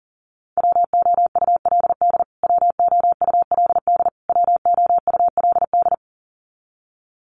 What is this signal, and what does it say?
Morse code, heard as a keyed tone, about 40 wpm, 700 Hz. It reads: WOULD WOULD WOULD
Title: Top 300 Words Repeated 40wpm